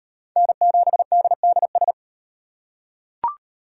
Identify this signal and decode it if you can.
Morse code, heard as a keyed tone, about 38 wpm, 700 Hz. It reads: N7DDS